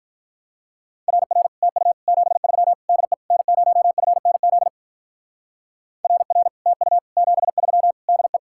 Morse code, heard as a keyed tone, about 43 wpm, 700 Hz. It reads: RR TU 73 DE N0FNZ RR TU 73 DE
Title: Speed-Racing Common QSO Phrases - 35wpm